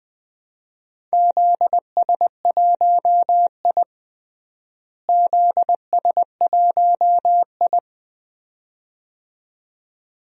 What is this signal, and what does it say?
Morse code, heard as a keyed tone, about 20 wpm, 700 Hz. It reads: ZS1I ZS1I
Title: DXCC-100 Call Signs - 4 Characters - Repeated 2x - 20wpm